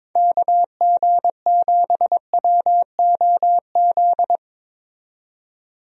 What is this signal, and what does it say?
Morse code, heard as a keyed tone, about 22 wpm, 700 Hz. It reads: KG7WOZ
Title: USA Call Signs - 6 Characters - 22wpm